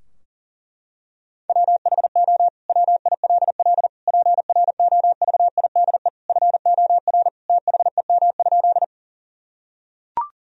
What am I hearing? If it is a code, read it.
Morse code, heard as a keyed tone, about 40 wpm, 700 Hz. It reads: WHO WILL PROVIDE FOR THEM?
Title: Sentences from Top 1000 Words - Review Part 3 - 40wpm